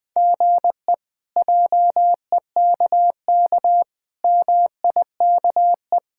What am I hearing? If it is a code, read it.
Morse code, heard as a keyed tone, about 20 wpm, 700 Hz. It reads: GE JEKK MIKE
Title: K1USN SST QSO Practice - 20wpm